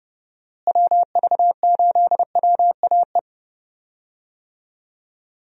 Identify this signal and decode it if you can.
Morse code, heard as a keyed tone, about 31 wpm, 700 Hz. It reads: WV8WAE
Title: USA Call Signs - 6 Characters - Repeated 2x - 30wpm